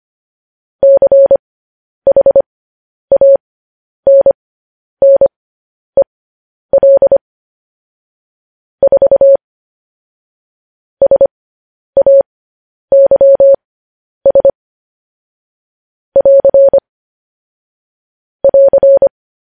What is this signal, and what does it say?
Morse code, heard as a keyed tone, about 25 wpm, spaced as for 10 wpm, 550 Hz. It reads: CHANNEL 4 SAYS <AR> <AR>